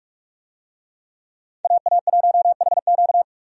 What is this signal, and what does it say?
Morse code, heard as a keyed tone, about 42 wpm, 700 Hz. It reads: AA1HQ
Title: USA Call Signs - 5 Characters - 45wpm